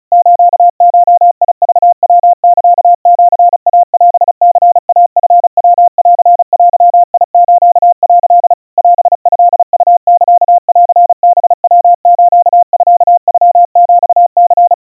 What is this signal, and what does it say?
Morse code (Japanese Wabun code), heard as a keyed tone, about 34 wpm, 700 Hz. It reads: スゴクヤサシイオニイチヤンデス」 オトウサンハヤスミノヒニ
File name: kawa / Morse sample / kamyu 130 cpm 700 Hz Crispy